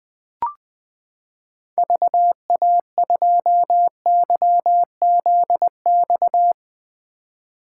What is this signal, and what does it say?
Morse code, heard as a keyed tone, about 20 wpm, 700 Hz. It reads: VA2YZX
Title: 2023 SOTA Chaser Call Signs - 20wpm